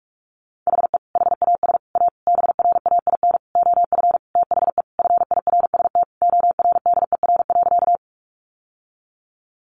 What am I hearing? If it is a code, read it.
Morse code, heard as a keyed tone, about 44 wpm, 700 Hz. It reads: HE HAS A BRAIN OF THE FIRST ORDER.
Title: Sentences from Top 1100 Words 45wpm